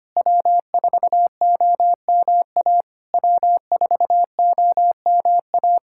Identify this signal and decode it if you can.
Morse code, heard as a keyed tone, about 25 wpm, 690 Hz. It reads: W4OMA W4OMA
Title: SOTA QSO Practice 25wpm